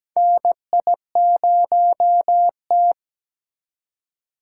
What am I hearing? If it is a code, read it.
Morse code, heard as a keyed tone, about 17 wpm, 700 Hz. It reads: NI0T
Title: USA Call Signs - 4x Character Spacing - 17wpm